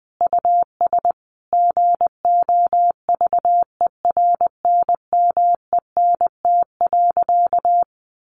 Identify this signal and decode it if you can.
Morse code, heard as a keyed tone, about 20 wpm, 700 Hz. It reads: US GOVERNMENT.
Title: Easy Phrases 20wpm